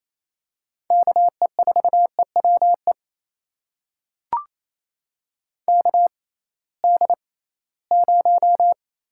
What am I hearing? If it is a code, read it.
Morse code, heard as a keyed tone, about 28 wpm, spaced as for 18 wpm, 700 Hz. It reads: KE4EWE KD0
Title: USA Call Signs - 6 Characters - 6x Character Spacing - 28wpm